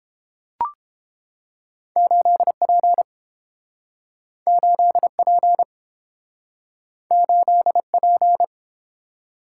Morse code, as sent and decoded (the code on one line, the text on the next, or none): ---.. .--. / ---.. .--. / ---.. .--.
8P 8P 8P